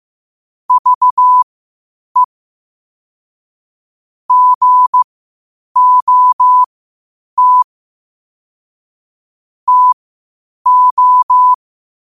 Morse code, heard as a keyed tone, about 15 wpm, spaced as for 8 wpm, 1000 Hz. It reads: VE GOT TO